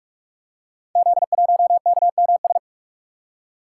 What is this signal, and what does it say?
Morse code, heard as a keyed tone, about 44 wpm, 700 Hz. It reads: Z1KMS
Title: Code Groups - 5 Letter:Numbers - 45wpm